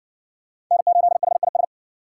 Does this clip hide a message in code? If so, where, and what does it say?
Yes: Morse code from 0.7 to 1.6 s, heard as a keyed tone, about 53 wpm, 700 Hz. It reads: N8HIS